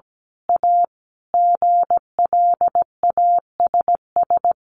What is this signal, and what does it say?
Morse code, heard as a keyed tone, about 17 wpm, 700 Hz. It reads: A GLASS
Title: Sentences from Top 1300 Words 17wpm